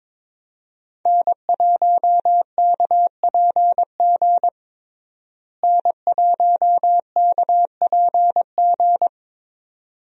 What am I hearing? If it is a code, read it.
Morse code, heard as a keyed tone, about 22 wpm, 700 Hz. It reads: N1KPG N1KPG
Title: USA Call Signs - 5 Characters - Repeated 2x - 22wpm